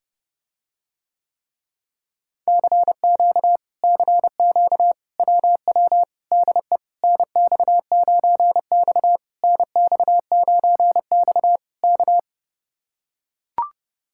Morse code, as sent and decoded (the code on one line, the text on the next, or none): -.-. --.- / -.-. --.- / .-- .-- / -.. / -. -..- ----. -..- / -. -..- ----. -..- / -.-
CQ CQ WW DE NX9X NX9X K